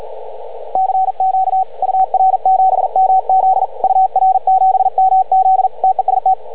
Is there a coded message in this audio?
Yes: Morse code from 0.8 to 6.3 s, heard as a keyed tone, about 45 wpm, 750 Hz, with radio noise. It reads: CQ UR7MZ UR7MZ TEST